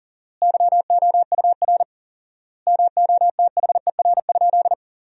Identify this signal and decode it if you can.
Morse code, heard as a keyed tone, about 40 wpm, 700 Hz. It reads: YOUR MOTHER?
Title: 4x Word-Spacing - Sentences from Top 200 Words - 40wpm